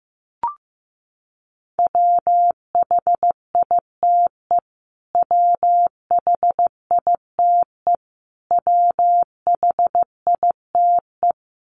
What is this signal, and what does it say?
Morse code, heard as a keyed tone, about 15 wpm, 700 Hz. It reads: WHITE WHITE WHITE
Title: Top 500 Words Repeated 15wpm